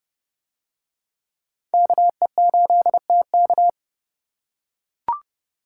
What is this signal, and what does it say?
Morse code, heard as a keyed tone, about 29 wpm, 700 Hz. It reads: KE8TK